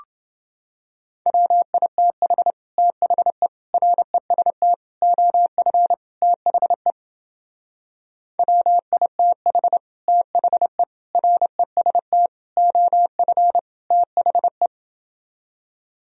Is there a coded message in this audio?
Yes: Morse code from 1.3 to 14.7 s, heard as a keyed tone, about 30 wpm, 700 Hz. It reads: WITH THE REST OF THE WITH THE REST OF THE